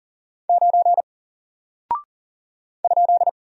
Morse code, heard as a keyed tone, about 40 wpm, 700 Hz. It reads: 9 ?
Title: Single Character (Letter:Number:Punctuation:Prosign) Alt 40wpm